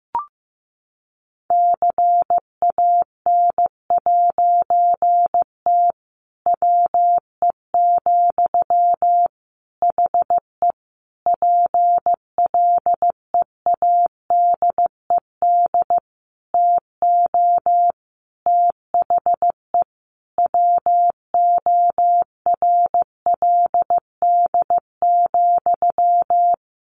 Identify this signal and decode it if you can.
Morse code, heard as a keyed tone, about 15 wpm, 700 Hz. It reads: CAN'T WE, HE PLEADED TO THE WORLD,
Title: A Sound Of Thunder 15wpm